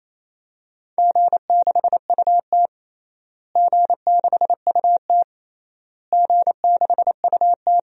Morse code, heard as a keyed tone, about 28 wpm, 700 Hz. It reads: G6UT G6UT G6UT
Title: DXCC-100 Call Signs - 4 Characters - Repeated 3x - 28wpm